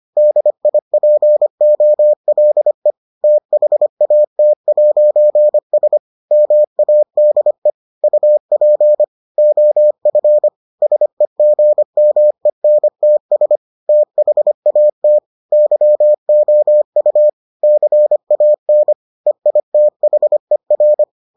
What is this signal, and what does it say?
Morse code, heard as a keyed tone, about 25 wpm, 600 Hz. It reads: DIPOLE THAT'S MADE UP OF SEGMENTS THAT YOU CAN EITHER